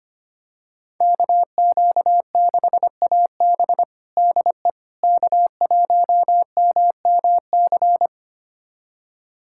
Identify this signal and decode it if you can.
Morse code, heard as a keyed tone, about 25 wpm, 700 Hz. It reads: KQ6AB DE K1MMC
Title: USA Callsign DE Callsign - 5 Characters - 25wpm